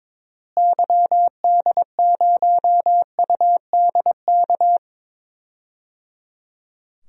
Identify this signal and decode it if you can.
Morse code, heard as a keyed tone, about 22 wpm, 700 Hz. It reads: YD0UDK
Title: DXCC-100 Call Signs - 6 Characters - Repeated 3x - 22wpm